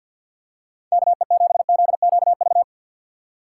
Morse code, heard as a keyed tone, about 47 wpm, 690 Hz. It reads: KE7ZQV
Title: USA Call Signs - 6x Character Spacing - 50wpm